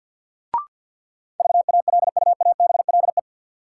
Morse code, heard as a keyed tone, about 48 wpm, 700 Hz. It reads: VALUABLE
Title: Sentences from Top 1100 Words - New Word Drill Alt - 50wpm